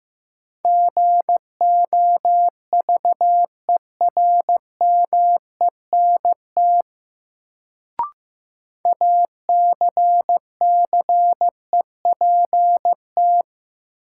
Morse code, heard as a keyed tone, about 15 wpm, 700 Hz. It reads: GOVERMENT ACCEPT
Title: Sentences from Top 1300 Words - New Word Drill - 15wpm